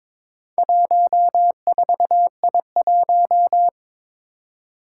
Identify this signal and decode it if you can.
Morse code, heard as a keyed tone, about 22 wpm, 700 Hz. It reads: 14I1
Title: Four Challenging Letter-Number Combinations 22wpm